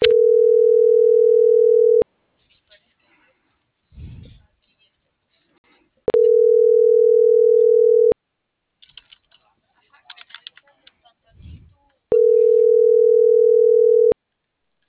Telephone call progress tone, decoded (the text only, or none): ringback tone